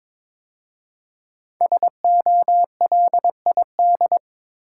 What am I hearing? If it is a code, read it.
Morse code, heard as a keyed tone, about 21 wpm, 700 Hz. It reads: SOLID
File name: 5 Letter Words - First 3 Letters Given - 22wpm